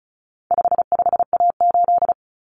Morse code, heard as a keyed tone, about 34 wpm, 700 Hz. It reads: 55A8
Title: Common Cut Numbers - 4-Digit 35wpm